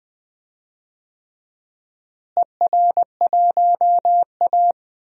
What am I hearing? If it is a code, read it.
Morse code, heard as a keyed tone, about 20 wpm, 700 Hz. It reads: ER1A
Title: DXCC-100 Call Signs - 4 Characters - Repeated 3x - 20wpm